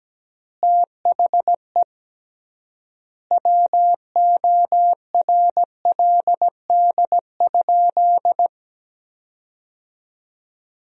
Morse code, heard as a keyed tone, about 17 wpm, 700 Hz. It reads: THE WORLD?
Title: 3x Word-Spacing - Sentences from Top 200 Words - 17wpm